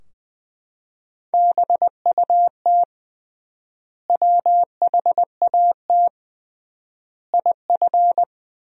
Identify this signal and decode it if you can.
Morse code, heard as a keyed tone, about 20 wpm, 700 Hz. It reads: BUT WHAT IF
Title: 3x Word-Spacing - Sentences from Top 100 Words - 20wpm